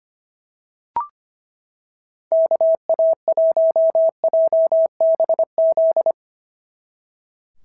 Morse code, heard as a keyed tone, about 24 wpm, 650 Hz. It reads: KA1JBZ
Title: SOTA QSO Practice 25wpm